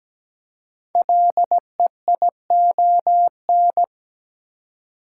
Morse code, heard as a keyed tone, about 17 wpm, 700 Hz. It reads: LEION
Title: Code Groups - 5 Letter:Numbers - 17wpm